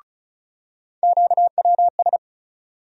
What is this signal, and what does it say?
Morse code, heard as a keyed tone, about 36 wpm, 700 Hz. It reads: QWS
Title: USA Call Sign Postfixes - 3 Letters 35wpm